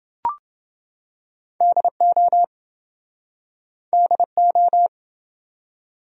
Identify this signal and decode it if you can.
Morse code, heard as a keyed tone, about 28 wpm, 700 Hz. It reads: DO DO